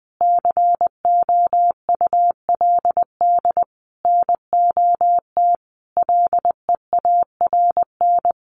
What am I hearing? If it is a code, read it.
Morse code, heard as a keyed tone, about 19 wpm, 700 Hz. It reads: COULD NOT LEARN
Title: Sentences from Top 1200 Words 20wpm